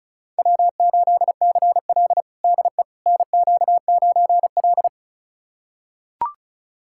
Morse code, heard as a keyed tone, about 36 wpm, 700 Hz. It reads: W8CL DE NQ9L
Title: USA Callsign DE Callsign - 4 Characters - 35wpm